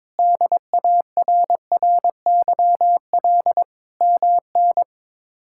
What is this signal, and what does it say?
Morse code, heard as a keyed tone, about 21 wpm, 700 Hz. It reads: DARRYL MN
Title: K1USN SST QSO Practice - 22wpm